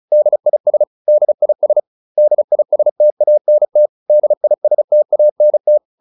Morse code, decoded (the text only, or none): DIS DIS DISTANT DISTANT